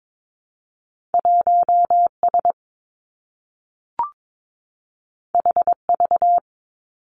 Morse code, heard as a keyed tone, about 22 wpm, 700 Hz. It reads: 1 S H V
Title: Two Challenging Letter-Number Combinations 22wpm